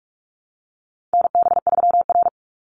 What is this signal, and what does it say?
Morse code, heard as a keyed tone, about 44 wpm, 700 Hz. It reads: NB3R
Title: Call Signs - 45wpm